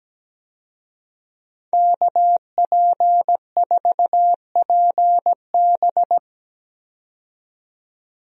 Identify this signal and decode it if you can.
Morse code, heard as a keyed tone, about 17 wpm, 700 Hz. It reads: KP4PB
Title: DXCC-100 Call Signs - 5 Characters - 17wpm